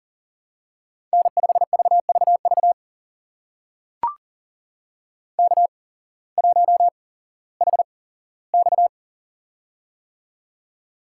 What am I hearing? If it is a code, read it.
Morse code, heard as a keyed tone, about 40 wpm, spaced as for 19 wpm, 700 Hz. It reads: N5VVV K1HX
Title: USA Call Signs - 8x Character Spacing - 40wpm